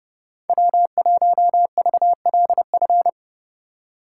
Morse code, heard as a keyed tone, about 30 wpm, 700 Hz. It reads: W1VLF